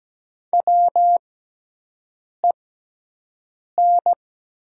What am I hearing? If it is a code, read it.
Morse code, heard as a keyed tone, about 17 wpm, spaced as for 6 wpm, 700 Hz. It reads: WEN
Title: 6x Character Spacing - Top 200 Words - 17wpm